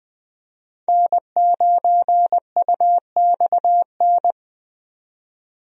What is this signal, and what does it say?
Morse code, heard as a keyed tone, about 21 wpm, 700 Hz. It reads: N9UXN
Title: USA Call Signs - 5 Characters - 20wpm